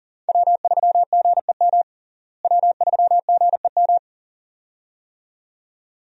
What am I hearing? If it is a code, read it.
Morse code, heard as a keyed tone, about 40 wpm, 700 Hz. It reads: W3GEM W3GEM
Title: USA Call Signs - 5 Characters - Repeated 2x - 40wpm